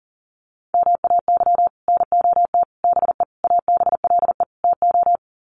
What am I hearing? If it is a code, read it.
Morse code, heard as a keyed tone, about 40 wpm, 700 Hz. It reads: MAY NOT BE ABLE TO
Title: Sets of 5 Words 40wpm